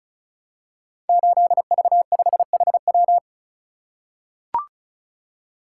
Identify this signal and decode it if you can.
Morse code, heard as a keyed tone, about 36 wpm, 700 Hz. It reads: 8V5HW
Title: Code Groups - 5 Letter:Numbers - 35wpm